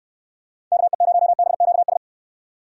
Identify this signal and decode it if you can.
Morse code, heard as a keyed tone, about 65 wpm, 700 Hz. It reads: KE0BQS